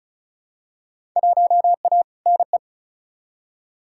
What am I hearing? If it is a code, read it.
Morse code, heard as a keyed tone, about 36 wpm, 700 Hz. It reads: 1A NE
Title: Field Day Exchanges Encore - 35wpm